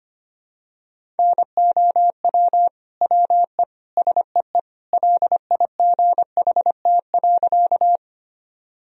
Morse code, heard as a keyed tone, about 25 wpm, 700 Hz. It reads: NOW WE SEE LIGHT.
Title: Sentences from Top 1000 Words - Review Part 1 - 25wpm